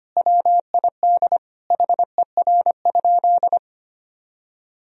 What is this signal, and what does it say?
Morse code, heard as a keyed tone, about 25 wpm, 700 Hz. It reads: WID HER?
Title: Sentences from Top 400 Words and Abbreviations 25wpm